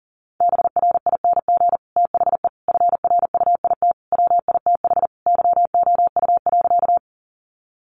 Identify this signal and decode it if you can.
Morse code, heard as a keyed tone, about 40 wpm, 700 Hz. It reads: BRING THE FRUIT WITH YOU.
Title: Sentences from Top 1300 Words 40wpm